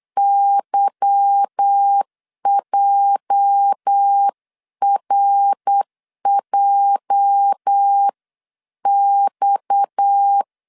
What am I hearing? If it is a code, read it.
Morse code, heard as a keyed tone, about 8 wpm, 800 Hz. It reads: YJRJX
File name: LEZIONI / 8 YJ45 / 8 YJ 8wpm 3313